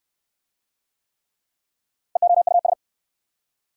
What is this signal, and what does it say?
Morse code, heard as a keyed tone, about 67 wpm, 700 Hz. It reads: EOUS